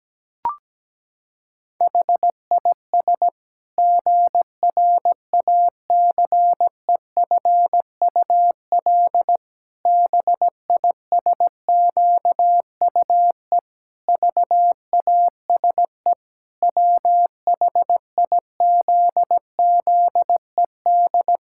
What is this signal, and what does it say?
Morse code, heard as a keyed tone, about 17 wpm, 700 Hz. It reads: HIS GRACEFUL BISQUE VASE WHIZZED
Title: Pangram Challenge v2 - 17wpm